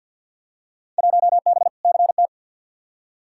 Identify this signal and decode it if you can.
Morse code, heard as a keyed tone, about 52 wpm, 700 Hz. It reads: 1B CT
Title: Field Day Exchanges Encore - 50wpm